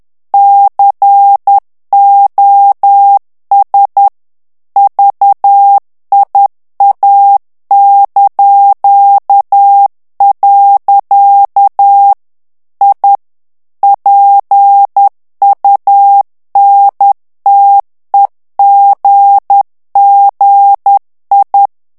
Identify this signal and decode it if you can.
Morse code, heard as a keyed tone, about 10 wpm, 800 Hz. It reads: COS VIA). I PUNTEGGI
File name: ITALIEN texte clair 2 vitesse 10 mots minute